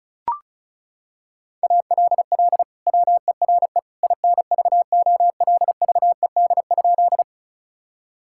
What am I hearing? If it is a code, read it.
Morse code, heard as a keyed tone, about 34 wpm, 700 Hz. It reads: ALL WERE INVOLVED?
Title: Sentences from Top 1000 Words - Review Part 5 - 35wpm